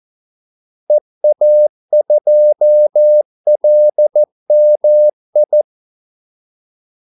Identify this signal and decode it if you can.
Morse code, heard as a keyed tone, about 14 wpm, 600 Hz. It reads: EA2LMI
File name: Callsigns-14wpm